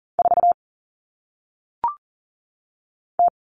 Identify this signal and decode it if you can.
Morse code, heard as a keyed tone, about 39 wpm, 700 Hz. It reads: 4 T
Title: Common Cut Numbers - 1-Digit 40wpm